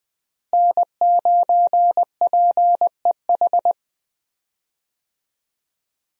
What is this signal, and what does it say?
Morse code, heard as a keyed tone, about 20 wpm, 700 Hz. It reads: N9PEH
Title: USA Call Signs - 5 Characters - Repeated 2x - 20wpm